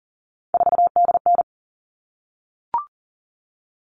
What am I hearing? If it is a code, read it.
Morse code, heard as a keyed tone, about 40 wpm, 700 Hz. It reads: 4DN
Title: All Cut Numbers - 3-Digit 40wpm